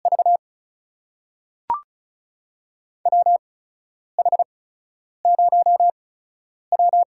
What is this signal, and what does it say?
Morse code, heard as a keyed tone, about 34 wpm, spaced as for 10 wpm, 700 Hz. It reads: V WH0W
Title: USA Call Signs - 8x Character Spacing - 35wpm